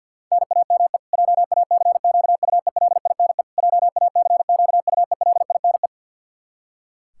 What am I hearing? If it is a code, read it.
Morse code, heard as a keyed tone, about 50 wpm, 700 Hz. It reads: NAME JACQUELINE JACQUELINE
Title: Practice QSO - Exchange 1 - Beginning - 50wpm